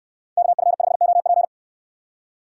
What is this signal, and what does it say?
Morse code, heard as a keyed tone, about 62 wpm, 690 Hz. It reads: KF5OP